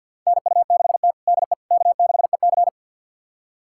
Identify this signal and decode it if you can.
Morse code, heard as a keyed tone, about 50 wpm, 700 Hz. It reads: NU6T DE K6EC
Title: USA Callsign DE Callsign - 4 Characters - 50wpm